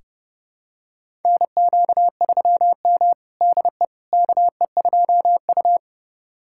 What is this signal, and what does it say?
Morse code, heard as a keyed tone, about 30 wpm, 700 Hz. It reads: NQ3M DE KE2U
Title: USA Callsign DE Callsign - 4 Characters - 30wpm